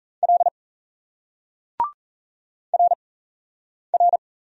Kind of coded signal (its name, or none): Morse code